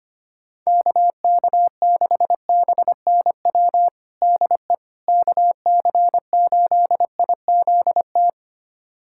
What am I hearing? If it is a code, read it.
Morse code, heard as a keyed tone, about 25 wpm, 700 Hz. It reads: KK6BNW DE KC8IZT
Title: USA Callsign DE Callsign 25wpm